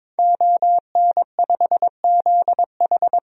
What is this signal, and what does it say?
Morse code, heard as a keyed tone, about 22 wpm, 700 Hz. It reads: ON5ZH
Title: DXCC-100 Call Signs - 5 Characters - 22wpm